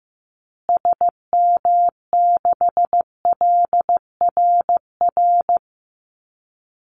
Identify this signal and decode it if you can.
Morse code, heard as a keyed tone, about 15 wpm, 700 Hz. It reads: SM6LRR
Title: Call Signs Repeated 2x - 15wpm